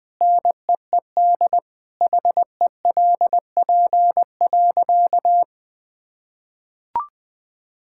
Morse code, heard as a keyed tone, about 20 wpm, 700 Hz. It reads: NEED HELP.
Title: Sentences with 3-Word Phrases - 20wpm